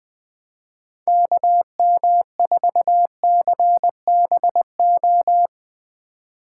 Morse code, heard as a keyed tone, about 20 wpm, 700 Hz. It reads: KM4CBO
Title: Speed Racing - USA 2x3 Call Signs 17wpm